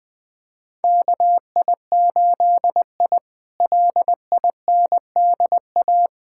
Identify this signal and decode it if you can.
Morse code, heard as a keyed tone, about 20 wpm, 700 Hz. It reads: KI8I LINDA
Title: K1USN SST QSO Practice - 20wpm